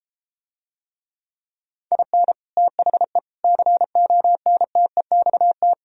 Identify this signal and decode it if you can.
Morse code, heard as a keyed tone, about 33 wpm, 700 Hz. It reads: IN THE CONTEXT